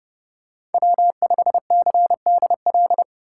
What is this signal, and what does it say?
Morse code, heard as a keyed tone, about 30 wpm, 700 Hz. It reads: W5CDL
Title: USA Call Signs - 5 Characters - 2x Character Spacing - 30wpm